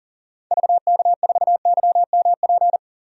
Morse code, heard as a keyed tone, about 40 wpm, 700 Hz. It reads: VK4YMP
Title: DXCC-100 Call Signs - 6 Characters - 40wpm